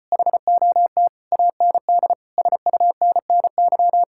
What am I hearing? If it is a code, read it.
Morse code, heard as a keyed tone, about 33 wpm, 700 Hz. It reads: HOT AND SUNNY